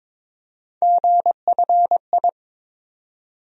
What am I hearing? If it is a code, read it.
Morse code, heard as a keyed tone, about 22 wpm, 700 Hz. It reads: GFI
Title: USA Call Sign Postfixes - 3 Letters 22wpm